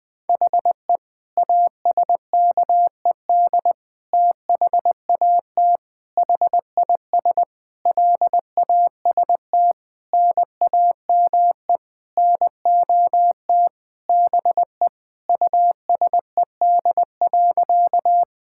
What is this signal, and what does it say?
Morse code, heard as a keyed tone, about 20 wpm, 700 Hz. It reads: HE ASKED THAT HIS LAST NAME NOT BE USED.
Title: Sentences with 9-Word Phrases - 20wpm